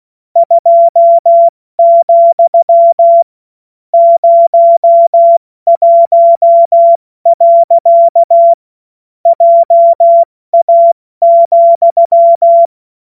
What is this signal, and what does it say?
Morse code, heard as a keyed tone, about 16 wpm, 680 Hz. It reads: 2, 01. JA,
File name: Morsekurs / Witze / Witz 21. Dezember 2021, 70 F80